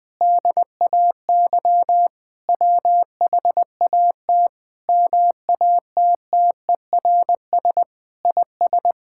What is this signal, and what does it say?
Morse code, heard as a keyed tone, about 21 wpm, 700 Hz. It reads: DAY WHAT MATTERS IS
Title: Sentences with 6-Word Phrases - 20wpm